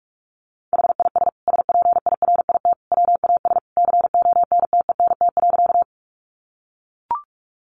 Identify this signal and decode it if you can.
Morse code, heard as a keyed tone, about 44 wpm, 700 Hz. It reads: HIS SPIRIT WAS CONTENT.
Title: Sentences from Top 1400 Words 45wpm